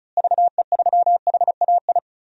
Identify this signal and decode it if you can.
Morse code, heard as a keyed tone, about 36 wpm, 700 Hz. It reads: VE3HAI
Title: DXCC-100 Call Signs - 6 Characters - Repeated 2x - 35wpm